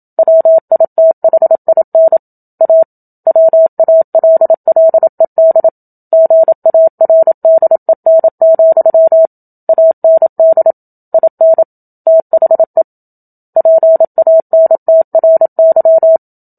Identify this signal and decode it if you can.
Morse code, heard as a keyed tone, about 27 wpm, 650 Hz. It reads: WITHIN A WALLED GARDEN, AND IN THE PANTRY
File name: Book Two / 01-Chapter 1 - 27 WPM